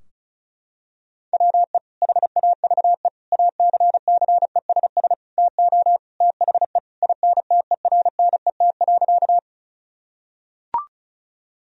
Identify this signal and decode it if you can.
Morse code, heard as a keyed tone, about 36 wpm, 700 Hz. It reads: WE HAVE ACCESS TO THE INTERNET.
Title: Sentences from Top 1000 Words - Review Part 4 - 35wpm